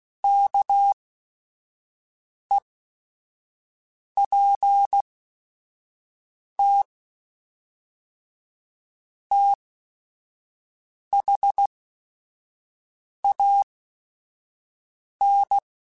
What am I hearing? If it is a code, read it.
Morse code, heard as a keyed tone, about 16 wpm, spaced as for 5 wpm, 800 Hz. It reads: KEPT THAN